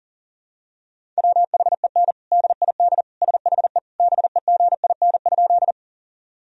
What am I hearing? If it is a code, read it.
Morse code, heard as a keyed tone, about 40 wpm, 700 Hz. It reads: WHEN DID SHE BEGIN?
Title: Sentences from Top 1000 Words - Review Part 1 - 40wpm